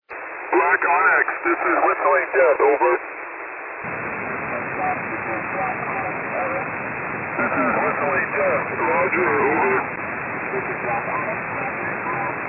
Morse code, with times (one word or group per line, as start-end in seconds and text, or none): none